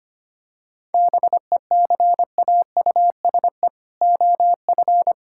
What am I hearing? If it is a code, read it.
Morse code, heard as a keyed tone, about 25 wpm, 700 Hz. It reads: BECAUSE OF
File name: practice / 25wpm / Sentences with 4-Word Phrases - 25wpm